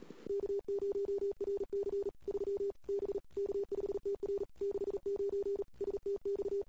Morse code, heard as a keyed tone, about 38 wpm, 400 Hz. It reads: K0RC 3 B K5TR 69 STX
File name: CW / 2008-ss-cw-k5tr-op-n5rz -21:00